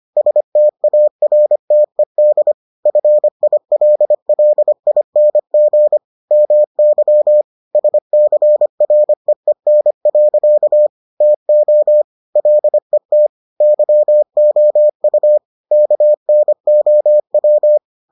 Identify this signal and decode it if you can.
Morse code, heard as a keyed tone, about 25 wpm, 600 Hz. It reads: STARTED FILLING MY SCREEN. TO LET YOU KNOW